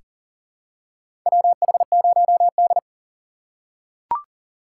Morse code, heard as a keyed tone, about 40 wpm, 700 Hz. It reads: WH0D
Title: USA Call Signs - 4 Characters - 4x Character Spacing - 40wpm